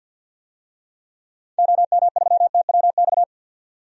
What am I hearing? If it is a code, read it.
Morse code, heard as a keyed tone, about 51 wpm, 700 Hz. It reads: OM3TWX
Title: DXCC-100 Call Signs - 6 Characters - 50wpm